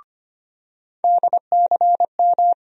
Morse code, heard as a keyed tone, about 25 wpm, 700 Hz. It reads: DCM
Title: USA Call Sign Postfixes - 3 Letters 25wpm